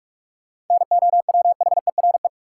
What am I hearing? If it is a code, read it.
Morse code, heard as a keyed tone, about 46 wpm, 700 Hz. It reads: NOWHERE
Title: Sets of 5 Words Encore - New Word Drill - 45wpm